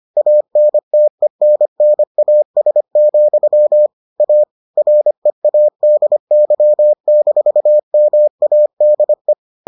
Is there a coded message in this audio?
Yes: Morse code from 0.2 to 9.3 s, heard as a keyed tone, about 25 wpm, 600 Hz. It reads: ANTENNAS, A READY-MADE